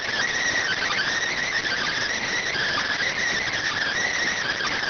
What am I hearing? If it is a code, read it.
Emergency Alert System (SAME) tones, heard as harsh buzzing bursts. It is heard from 0.6 to 4.2 s.